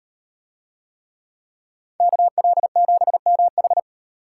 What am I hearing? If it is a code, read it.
Morse code, heard as a keyed tone, about 38 wpm, 700 Hz. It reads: KL7MH